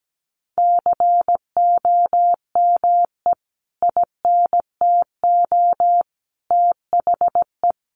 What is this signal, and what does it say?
Morse code, heard as a keyed tone, about 17 wpm, 700 Hz. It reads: COME INTO THE
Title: Sentences from Top 1400 Words 17wpm